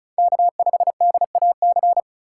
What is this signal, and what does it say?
Morse code, heard as a keyed tone, about 36 wpm, 700 Hz. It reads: K5DAC
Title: USA Call Signs - 5 Characters - 4x Character Spacing - 35wpm